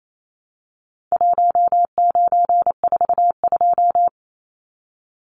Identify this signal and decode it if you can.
Morse code, heard as a keyed tone, about 28 wpm, 700 Hz. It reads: 1942
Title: Numbers - Contest Style - 4-Digit 28wpm